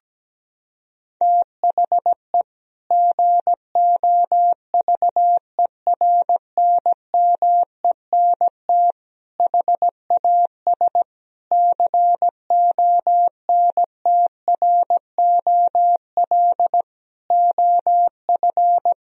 Morse code, heard as a keyed tone, about 17 wpm, 700 Hz. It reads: THE GOVERNMENT HAS CONTROL OF
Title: Sentences from Top 1000 Words - Review Part 3 - 17wpm